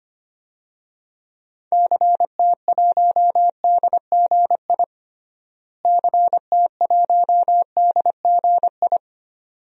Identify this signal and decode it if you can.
Morse code, heard as a keyed tone, about 25 wpm, 700 Hz. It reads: CT1DGI CT1DGI
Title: DXCC-100 Call Signs - 6 Characters - Repeated 2x - 25wpm